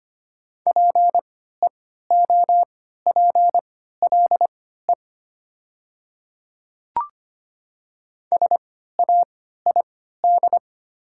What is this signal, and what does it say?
Morse code, heard as a keyed tone, about 25 wpm, spaced as for 14 wpm, 700 Hz. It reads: PEOPLE SAID
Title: 3x Character Spacing - Top 100 Words - 25wpm